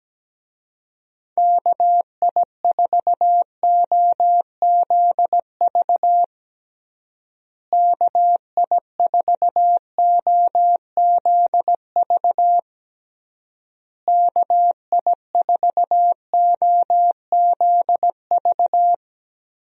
Morse code, heard as a keyed tone, about 17 wpm, 700 Hz. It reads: KI4OZV KI4OZV KI4OZV
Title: USA Call Signs - 6 Characters - Repeated 3x - 17wpm